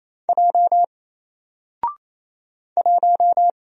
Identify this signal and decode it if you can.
Morse code, heard as a keyed tone, about 28 wpm, 700 Hz. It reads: J 1
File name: Single-Letter-Number-28wpm